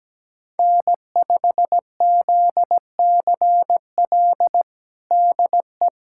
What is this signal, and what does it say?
Morse code, heard as a keyed tone, about 17 wpm, 700 Hz. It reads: N5ZCL DE
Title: USA Callsign DE Callsign 17wpm